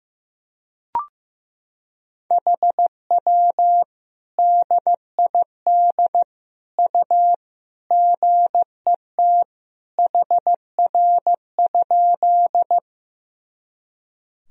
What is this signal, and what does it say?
Morse code, heard as a keyed tone, about 15 wpm, 700 Hz. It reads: HW DID U GET HR?
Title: Sentences from Top 200 Words and Abbreviations 15wpm